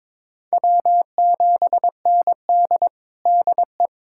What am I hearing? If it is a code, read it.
Morse code, heard as a keyed tone, about 22 wpm, 700 Hz. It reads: W7ND DE
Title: USA Callsign DE Callsign - 4 Characters - 22wpm